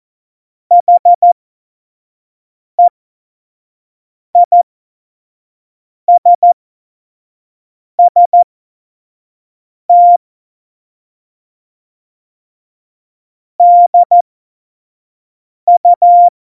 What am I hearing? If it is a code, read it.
Morse code, heard as a keyed tone, about 14 wpm, spaced as for 5 wpm, 700 Hz. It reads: HEISST DU
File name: Morsekurs / Witze / Witz 28. August 2018, 25 F70